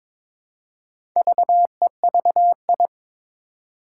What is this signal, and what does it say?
Morse code, heard as a keyed tone, about 22 wpm, 700 Hz. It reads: VEVI
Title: Four Challenging Letter-Number Combinations 22wpm